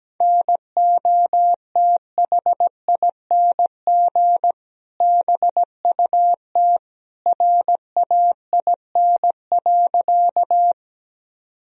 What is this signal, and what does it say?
Morse code, heard as a keyed tone, about 17 wpm, 700 Hz. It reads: NOTHING BUT RAIN.
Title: Sentences from Top 1000 Words - Review Part 1 - 17wpm